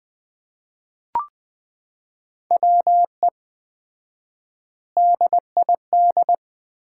Morse code, heard as a keyed tone, about 20 wpm, 700 Hz. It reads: WE DID
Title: 4x Word-Spacing - Sets of 4 Words - 20wpm